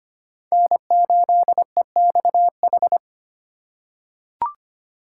Morse code, heard as a keyed tone, about 25 wpm, 700 Hz. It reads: N8EXH
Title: USA Call Signs - 4x Character Spacing - 25wpm